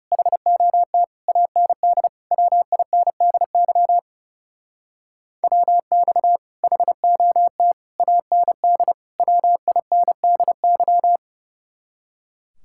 Morse code, heard as a keyed tone, about 35 wpm, 700 Hz. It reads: HOT AND WINDY WX HOT AND WINDY
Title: Speed-Racing Common QSO Phrases - 30wpm